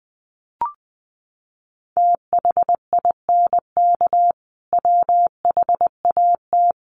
Morse code, heard as a keyed tone, about 20 wpm, 700 Hz. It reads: THINK WHAT
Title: Sentences from Top 200 Words 20wpm